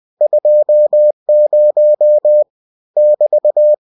Morse code, heard as a keyed tone, about 20 wpm, 600 Hz. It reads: 20 <BT>
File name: cw-books / winnie-the-pooh / winnie-20wpm-page-0020